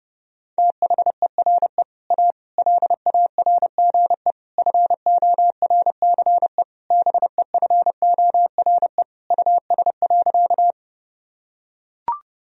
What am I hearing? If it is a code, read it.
Morse code, heard as a keyed tone, about 30 wpm, 700 Hz. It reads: THERE A LARGE FORCE BEFORE US.
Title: Sentences from Top 1000 Words - Review Part 3 - 30wpm